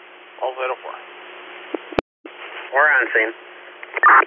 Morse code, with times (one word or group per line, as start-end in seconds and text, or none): none